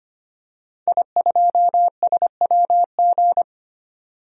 Morse code, heard as a keyed tone, about 25 wpm, 700 Hz. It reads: I2SWG
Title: DXCC-100 Call Signs - 5 Characters - 25wpm